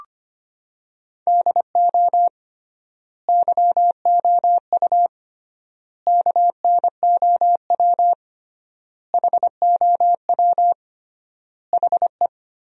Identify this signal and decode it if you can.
Morse code, heard as a keyed tone, about 25 wpm, 700 Hz. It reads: DO YOU KNOW HOW HE
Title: 3x Word-Spacing - Sentences from Top 300 Words - 25wpm